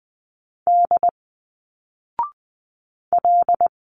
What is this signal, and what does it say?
Morse code, heard as a keyed tone, about 20 wpm, 700 Hz. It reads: D L
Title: Single Letters 20wpm